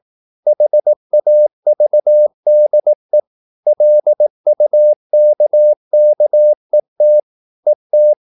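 Morse code, heard as a keyed tone, about 18 wpm, 600 Hz. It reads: HAVDE LUKKET ET
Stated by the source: Slow (QRS) Morse practice at 18 wpm, CW med Gnister.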